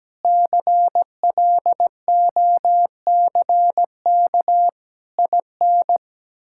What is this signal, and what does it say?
Morse code, heard as a keyed tone, about 17 wpm, 700 Hz. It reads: CLOCK IN
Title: Pangram Challenge v2 - 17wpm